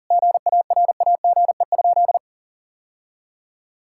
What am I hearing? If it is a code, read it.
Morse code, heard as a keyed tone, about 40 wpm, 700 Hz. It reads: GARAGE?
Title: Sentences from Top 1600 Words 40wpm